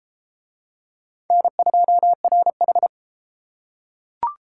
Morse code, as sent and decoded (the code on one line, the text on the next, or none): -. ..--- .-. ....
N2RH